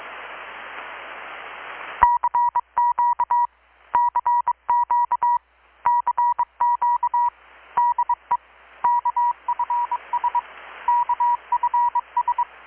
Morse code, heard as a keyed tone, about 21 wpm, 960 Hz. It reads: CQ CQ CQ DE KFS KFS
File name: blog / swl / 6-abril-2019-12695-USB-KPS